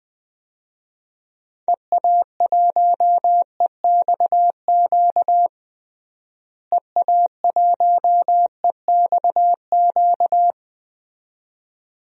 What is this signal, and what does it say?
Morse code, heard as a keyed tone, about 20 wpm, 700 Hz. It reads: EA1EXQ EA1EXQ
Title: DXCC-100 Call Signs - 6 Characters - Repeated 2x - 20wpm